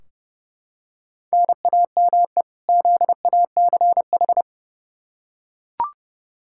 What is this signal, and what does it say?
Morse code, heard as a keyed tone, about 30 wpm, 700 Hz. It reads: NAME ZACH
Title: K1USN SST QSO Practice 30wpm